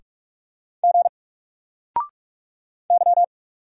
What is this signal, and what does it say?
Morse code, heard as a keyed tone, about 44 wpm, 700 Hz. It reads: G Y
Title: Single Character (Letter:Number:Punctuation:Prosign) Alt 45wpm